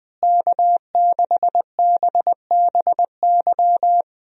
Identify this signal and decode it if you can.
Morse code, heard as a keyed tone, about 21 wpm, 700 Hz. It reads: K6BBY
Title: USA Call Signs - 5 Characters - Repeated 2x - 20wpm